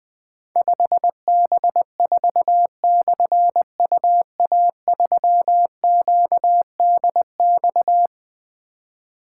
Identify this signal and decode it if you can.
Morse code, heard as a keyed tone, about 20 wpm, 700 Hz. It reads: 5B4/UA3QDX
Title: Difficult Call Signs 20wpm